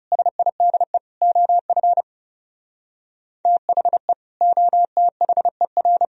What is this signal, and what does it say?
Morse code, heard as a keyed tone, about 35 wpm, 700 Hz. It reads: SIDE OF THE OTHER
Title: Speed-Racing - Sets of 4 Words 30wpm